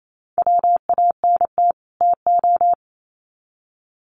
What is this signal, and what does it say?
Morse code, heard as a keyed tone, about 28 wpm, 700 Hz. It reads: WANT TO
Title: Sets of 2 Words 28wpm